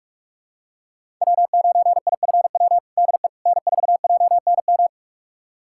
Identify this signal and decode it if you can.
Morse code, heard as a keyed tone, about 41 wpm, 700 Hz. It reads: W9IFW DE N4JNM
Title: USA Callsign DE Callsign - 5 Characters - 45wpm